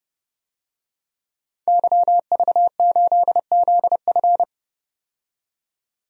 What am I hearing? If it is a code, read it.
Morse code, heard as a keyed tone, about 31 wpm, 700 Hz. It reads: YV8ZF
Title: DXCC-100 Call Signs - 5 Characters - 30wpm